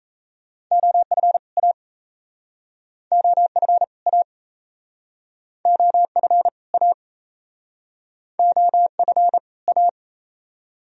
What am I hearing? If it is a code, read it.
Morse code, heard as a keyed tone, about 40 wpm, 700 Hz. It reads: OF A OF A OF A OF A